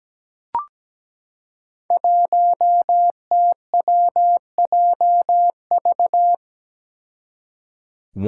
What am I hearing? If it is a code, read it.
Morse code, heard as a keyed tone, about 17 wpm, 700 Hz. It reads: 1TWJV
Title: Five Challenging Letter-Number Combinations 17wpm